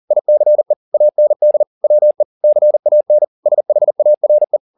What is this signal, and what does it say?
Morse code, heard as a keyed tone, about 40 wpm, 600 Hz. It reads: ICE AND WE CAN SHARE